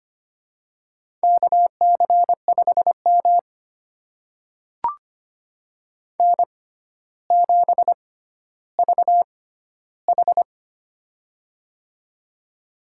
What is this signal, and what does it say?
Morse code, heard as a keyed tone, about 25 wpm, spaced as for 13 wpm, 700 Hz. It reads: KC5M N7VH
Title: USA Call Signs - 4 Characters - 6x Character Spacing - 25wpm